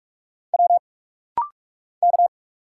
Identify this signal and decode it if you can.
Morse code, heard as a keyed tone, about 44 wpm, 700 Hz. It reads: W K
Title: Single Letter-Number - Rapid-Fire 45wpm